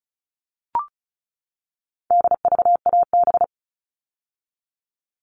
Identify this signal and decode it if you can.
Morse code, heard as a keyed tone, about 34 wpm, 700 Hz. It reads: DVAB
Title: All Cut Numbers - 4-Digit 35wpm